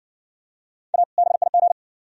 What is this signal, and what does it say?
Morse code, heard as a keyed tone, about 60 wpm, 700 Hz. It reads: A BIG